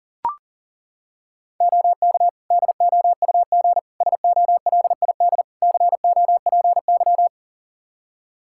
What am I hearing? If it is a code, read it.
Morse code, heard as a keyed tone, about 40 wpm, 700 Hz. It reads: OK DOUG SOLID COPY
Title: K1USN SST QSO Practice 40wpm